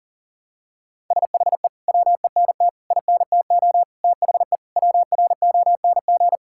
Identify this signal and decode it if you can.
Morse code, heard as a keyed tone, about 40 wpm, 700 Hz. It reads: SHE WENT INTO THE WRONG
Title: Sentences from Top 1600 Words 40wpm